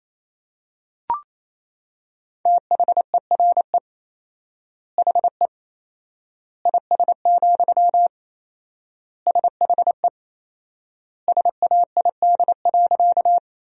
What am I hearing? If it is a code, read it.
Morse code, heard as a keyed tone, about 27 wpm, 700 Hz. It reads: THERE HE IS, SHE SAID.
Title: 4x Word-Spacing - Sentences from Top 100 Words - 28wpm